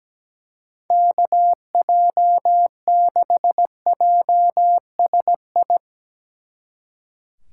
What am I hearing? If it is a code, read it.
Morse code, heard as a keyed tone, about 19 wpm, 700 Hz. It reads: KJ6JSI
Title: Speed Racing - USA 2x3 Call Signs 17wpm